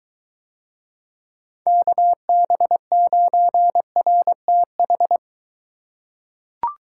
Morse code, heard as a keyed tone, about 23 wpm, 700 Hz. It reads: KB9RTH